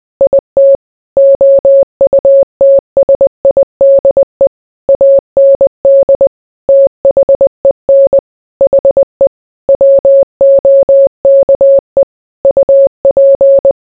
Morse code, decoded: IT OUTSIDE AND THEN HE WOKE UP